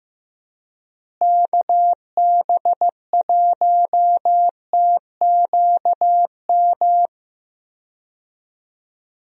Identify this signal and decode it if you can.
Morse code, heard as a keyed tone, about 15 wpm, 700 Hz. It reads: KB1TQM
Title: USA Call Signs - 6 Characters - Repeated 3x - 15wpm